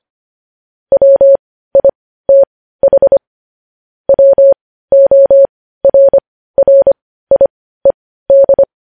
Morse code, heard as a keyed tone, about 25 wpm, spaced as for 15 wpm, 550 Hz. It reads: WITH WORRIED
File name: podcast / News headlines-1x-25x15